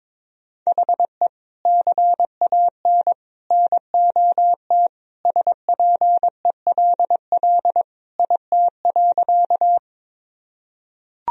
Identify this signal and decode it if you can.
Morse code, heard as a keyed tone, about 22 wpm, 700 Hz. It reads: HE CAN NOT SPELL IT.
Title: Sentences from Top 1000 Words - Review Part 1 - 22wpm